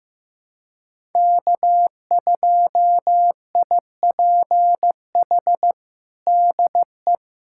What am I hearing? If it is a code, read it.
Morse code, heard as a keyed tone, about 15 wpm, 700 Hz. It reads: K2IPH DE
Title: USA Callsign DE Callsign - 5 Characters - 15wpm